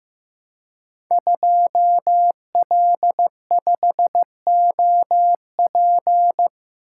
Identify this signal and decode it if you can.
Morse code, heard as a keyed tone, about 15 wpm, 700 Hz. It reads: 2L5OP
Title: Code Groups - 5 Letter:Numbers - 15wpm